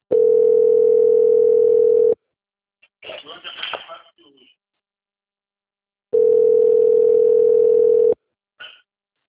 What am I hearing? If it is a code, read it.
Ringback tone (North America), 440 + 480 Hz.